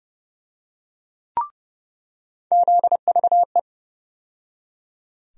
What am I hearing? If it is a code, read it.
Morse code, heard as a keyed tone, about 30 wpm, 700 Hz. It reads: ZVE
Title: USA Call Sign Postfixes - 3 Letters 30wpm